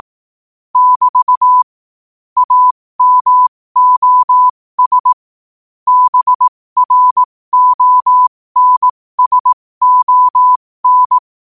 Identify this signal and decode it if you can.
Morse code, heard as a keyed tone, about 18 wpm, spaced as for 15 wpm, 1000 Hz. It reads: <BT> AMOS BRONSON
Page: New quotes every day in morse code at 15 Words per minute.